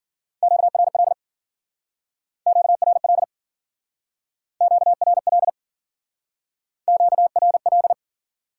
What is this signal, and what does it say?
Morse code, heard as a keyed tone, about 52 wpm, 700 Hz. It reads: QRL QRL QRL QRL